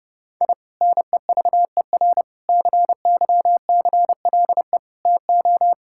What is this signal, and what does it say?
Morse code, heard as a keyed tone, about 32 wpm, 700 Hz. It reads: I NEVER CYCLE TO
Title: Sentences from Top 1000 Words - Review Part 6 - 30wpm